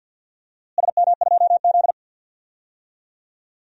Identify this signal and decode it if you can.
Morse code, heard as a keyed tone, about 52 wpm, 700 Hz. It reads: SM2Z
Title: DXCC-100 Call Signs - 4 Characters - Repeated 3x - 50wpm